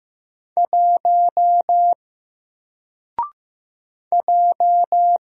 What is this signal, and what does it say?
Morse code, heard as a keyed tone, about 15 wpm, 700 Hz. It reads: 1 J
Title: Single Character (Letter:Number:Punctuation:Prosign) 15wpm